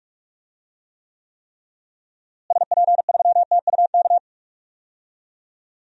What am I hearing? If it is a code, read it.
Morse code, heard as a keyed tone, about 46 wpm, 700 Hz. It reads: SP3TUK
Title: DXCC-100 Call Signs - 6 Characters - Repeated 3x - 45wpm